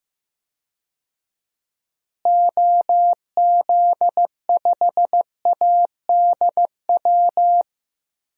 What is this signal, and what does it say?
Morse code, heard as a keyed tone, about 15 wpm, 700 Hz. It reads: OZ5ADW
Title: DXCC-100 Call Signs - 6 Characters - Repeated 3x - 15wpm